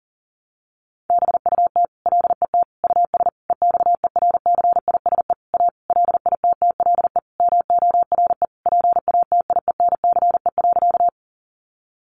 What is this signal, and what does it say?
Morse code, heard as a keyed tone, about 40 wpm, 700 Hz. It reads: BUT LET US EXERCISE A LITTLE MORE PATIENCE.
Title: Sentences from Top 1300 Words 40wpm